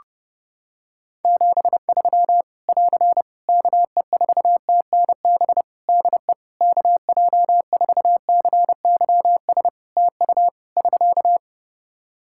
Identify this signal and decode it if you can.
Morse code, heard as a keyed tone, about 30 wpm, 700 Hz. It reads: 73 <AR> KE4TNB DE KJ4CYS TU <SK>
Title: Practice QSO - Exchange 4 - Ending - 30wpm